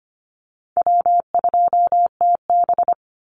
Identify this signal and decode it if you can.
Morse code, heard as a keyed tone, about 25 wpm, 700 Hz. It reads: W2TB
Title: Easy Call Signs - 25wpm